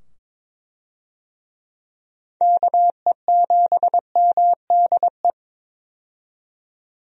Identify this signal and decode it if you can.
Morse code, heard as a keyed tone, about 22 wpm, 700 Hz. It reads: KE7MDE
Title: USA Call Signs - 6 Characters - 22wpm